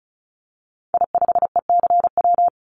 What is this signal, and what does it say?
Morse code, heard as a keyed tone, about 36 wpm, 700 Hz. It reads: I5ECW
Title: Call Signs - 35wpm